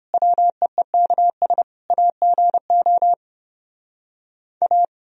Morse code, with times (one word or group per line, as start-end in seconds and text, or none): 0.1-1.6 WEEKS
1.9-3.1 AGO
4.6-4.8 A